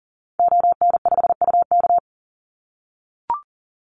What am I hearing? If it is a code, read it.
Morse code, heard as a keyed tone, about 40 wpm, 700 Hz. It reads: ON5UK
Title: Call Signs - 40wpm